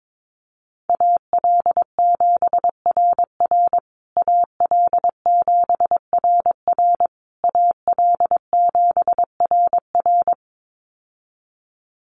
Morse code, heard as a keyed tone, about 22 wpm, 700 Hz. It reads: AL7RR AL7RR AL7RR
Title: USA Call Signs Repeated 3x 22wpm